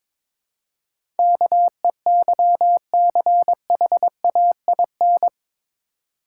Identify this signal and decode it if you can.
Morse code, heard as a keyed tone, about 23 wpm, 700 Hz. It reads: KEYCHAIN
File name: Difficult Words with Hints - 22wpm